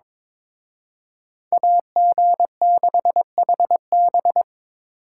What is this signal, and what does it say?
Morse code, heard as a keyed tone, about 26 wpm, 700 Hz. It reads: AG6HB
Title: Speed Racing - USA 2x2 Call Signs 22wpm